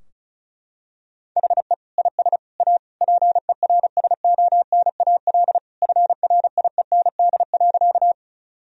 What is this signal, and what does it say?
Morse code, heard as a keyed tone, about 36 wpm, 700 Hz. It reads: HE IS A PERSONAL FRIEND.
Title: Sentences from Top 1000 Words - Review Part 4 - 35wpm